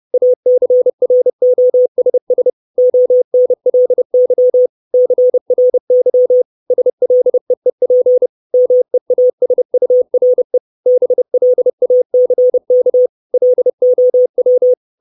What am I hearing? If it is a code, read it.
Morse code, heard as a keyed tone, about 30 wpm, 500 Hz. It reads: ACROSS ONLY CRY SLEEP MEASURE BLACK LOW
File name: 632 Words 30WPM